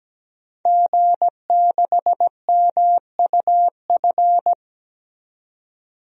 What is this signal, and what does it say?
Morse code, heard as a keyed tone, about 17 wpm, 700 Hz. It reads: G6MUF
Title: DXCC-100 Call Signs - 5 Characters - 17wpm